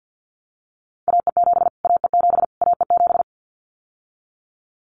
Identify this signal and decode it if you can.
Morse code, heard as a keyed tone, about 50 wpm, 700 Hz. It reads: AE7 AE7 AE7
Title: Callsign Prefixes for Top 25 DXCC - Repeated 3x - 50wpm